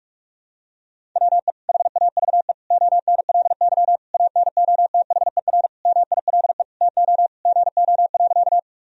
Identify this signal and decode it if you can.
Morse code, heard as a keyed tone, about 46 wpm, 700 Hz. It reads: WE HAVE ONLY ANOTHER MILE TO GO.
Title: Sentences from Top 1000 Words - Review Part 1 - 45wpm